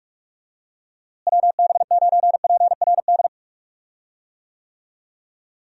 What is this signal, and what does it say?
Morse code, heard as a keyed tone, about 43 wpm, 700 Hz. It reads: WB9PRD
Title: USA Call Signs - 6 Characters - Repeated 2x - 45wpm